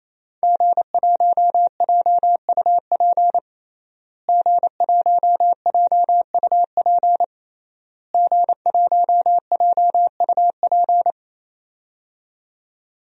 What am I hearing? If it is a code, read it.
Morse code, heard as a keyed tone, about 27 wpm, 700 Hz. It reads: G1JUP G1JUP G1JUP
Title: DXCC-100 Call Signs - 5 Characters - Repeated 3x - 28wpm